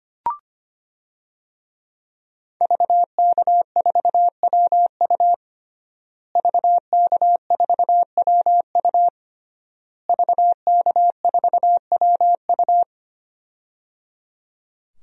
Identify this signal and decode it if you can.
Morse code, heard as a keyed tone, about 25 wpm, 700 Hz. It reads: VK4WU VK4WU VK4WU
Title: DXCC-100 Call Signs - 5 Characters - Repeated 3x - 25wpm